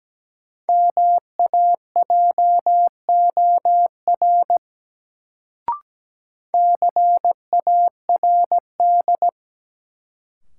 Morse code, heard as a keyed tone, about 17 wpm, 700 Hz. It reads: MAJOR CARD
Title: Sentences from Top 600 Words - New Word Drill - 17wpm